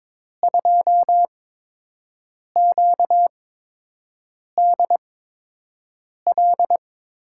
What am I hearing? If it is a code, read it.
Morse code, heard as a keyed tone, about 22 wpm, spaced as for 6 wpm, 700 Hz. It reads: 2QDL
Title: USA Call Signs - 8x Character Spacing - 22wpm